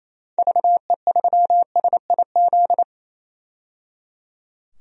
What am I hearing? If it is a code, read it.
Morse code, heard as a keyed tone, about 27 wpm, 700 Hz. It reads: VE3SIZ
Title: DXCC-100 Call Signs - 6 Characters - 28wpm